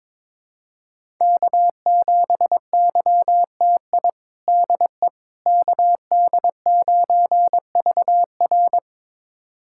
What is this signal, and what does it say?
Morse code, heard as a keyed tone, about 22 wpm, 700 Hz. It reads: K7YTI DE KD9VR
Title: USA Callsign DE Callsign - 5 Characters - 22wpm